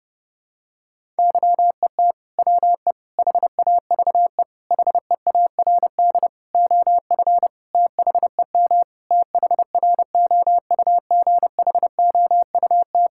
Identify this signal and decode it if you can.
Morse code, heard as a keyed tone, about 30 wpm, 700 Hz. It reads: YET WE HAVE HEARD OF THEM THROUGHOUT
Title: Sentences from Top 1000 Words - Review Part 5 - 30wpm